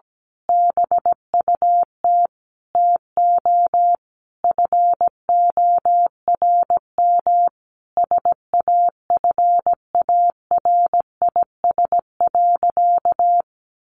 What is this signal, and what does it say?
Morse code, heard as a keyed tone, about 17 wpm, 700 Hz. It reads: BUT TO FORM SAFARIS.
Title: A Sound Of Thunder 17wpm